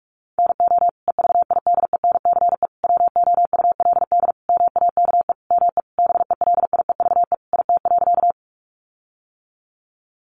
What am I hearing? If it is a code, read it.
Morse code, heard as a keyed tone, about 44 wpm, 700 Hz. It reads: NO EVIDENCE WOULD MAKE ME BELIEVE IT.
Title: Sentences from Top 1300 Words 45wpm